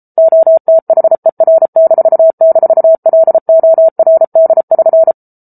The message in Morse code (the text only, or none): OTHER--LORD<SN>